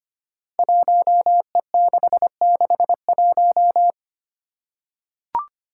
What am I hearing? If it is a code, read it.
Morse code, heard as a keyed tone, about 25 wpm, 700 Hz. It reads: 1E661
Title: Five Challenging Letter-Number Combinations 25wpm